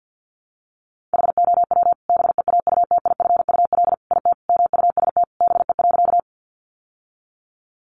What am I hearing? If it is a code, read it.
Morse code, heard as a keyed tone, about 50 wpm, 700 Hz. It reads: HOW BEAUTIFUL IT MUST BE.
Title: Sentences from Top 900 Words 50wpm